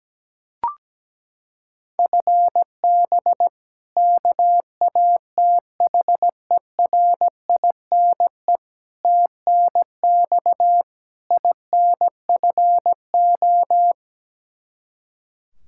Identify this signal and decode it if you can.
Morse code, heard as a keyed tone, about 17 wpm, 700 Hz. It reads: FB KATHERINE TNX INFO
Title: K1USN SST QSO Practice 17wpm